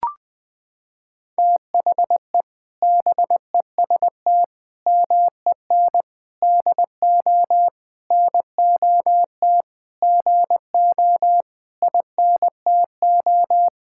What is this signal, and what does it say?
Morse code, heard as a keyed tone, about 20 wpm, 700 Hz. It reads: THE BEST MEN DO NOT GO INTO
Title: Sentences from Top 1000 Words - Review Part 6 - 20wpm